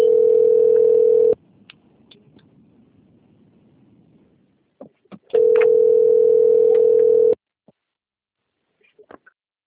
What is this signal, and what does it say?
Ringback tone (North America), 440 + 480 Hz.